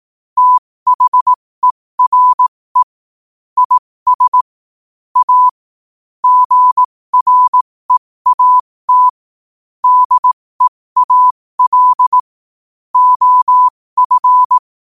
Morse code, heard as a keyed tone, about 18 wpm, spaced as for 15 wpm, 1000 Hz. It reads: THERE IS A GREAT DEAL OF
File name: qotdfiles / 15wpm